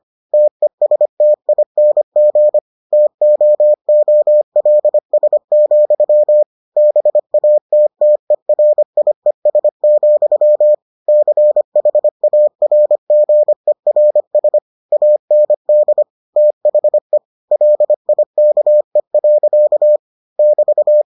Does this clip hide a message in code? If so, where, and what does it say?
Yes: Morse code from 0.3 to 21.0 s, heard as a keyed tone, about 25 wpm, 600 Hz. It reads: TESTING TOOLS, BATTERIES, CHARGERS AND THE LIKE. <BT>